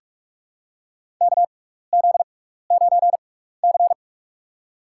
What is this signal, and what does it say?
Morse code, heard as a keyed tone, about 45 wpm, spaced as for 15 wpm, 700 Hz. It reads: KZ9C
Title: USA Call Signs - 4 Characters - 6x Character Spacing - 45wpm